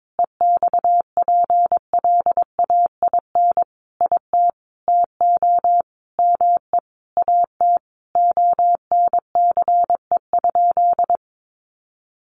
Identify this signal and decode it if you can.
Morse code, heard as a keyed tone, about 22 wpm, 700 Hz. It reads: EXPLAIN IT TO ME AT ONCE?
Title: Sentences from Top 1100 Words 22wpm